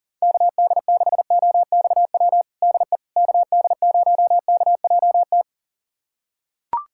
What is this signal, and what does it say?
Morse code, heard as a keyed tone, about 40 wpm, 700 Hz. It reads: KD6OXW DE KD0KJT
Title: USA Callsign DE Callsign - 6 Characters - 40wpm